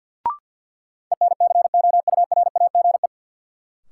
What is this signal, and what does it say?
Morse code, heard as a keyed tone, about 52 wpm, 700 Hz. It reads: ENCOURAGE
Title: Sentences from Top 1400 Words - New Word Drill Alt - 50wpm